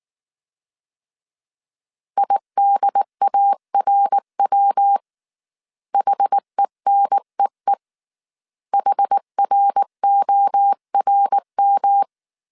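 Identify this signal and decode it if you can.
Morse code, heard as a keyed tone, about 19 wpm, 800 Hz. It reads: IDARW HENEE HRORM
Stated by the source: Morse code dictation at 18 wpm from the IW3SGT QRS Morse code training course, five-character groups of 300 characters.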